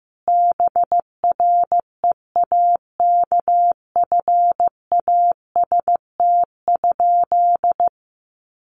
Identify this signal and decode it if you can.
Morse code, heard as a keyed tone, about 15 wpm, 700 Hz. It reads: BREAKFAST?
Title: Sentences from Top 1500 Words 15wpm